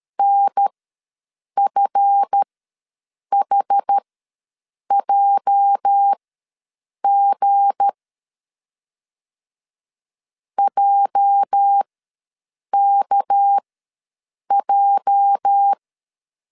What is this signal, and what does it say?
Morse code, heard as a keyed tone, about 13 wpm, spaced as for 7 wpm, 800 Hz. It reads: NFHJG JKJ
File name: LEZIONI / 8 YJ45 / 8 YJ ws 12wpm 3342